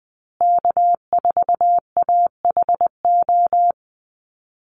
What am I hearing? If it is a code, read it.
Morse code, heard as a keyed tone, about 20 wpm, 700 Hz. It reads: K4AHO
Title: Call Signs Repeated 2x - 20wpm